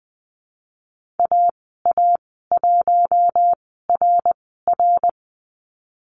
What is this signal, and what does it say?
Morse code, heard as a keyed tone, about 20 wpm, spaced as for 15 wpm, 700 Hz. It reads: AA1RR
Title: 167 Lesson 27g - Number 7 - Call Signs 20wpm